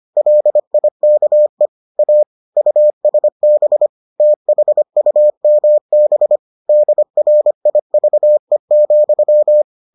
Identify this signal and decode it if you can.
Morse code, heard as a keyed tone, about 25 wpm, 600 Hz. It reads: LIKE A USB THUMB DRIVE,